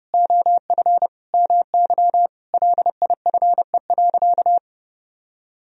Morse code, heard as a keyed tone, about 32 wpm, 700 Hz. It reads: OF MY LIFE.
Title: Sentences from Top 1000 Words - Review Part 2 - 30wpm